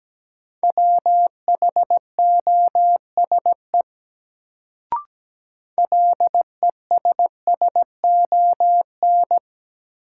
Morse code, heard as a keyed tone, about 17 wpm, 700 Hz. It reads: WHOSE LESSON
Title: Sentences from Top 1200 Words - New Word Drill - 17wpm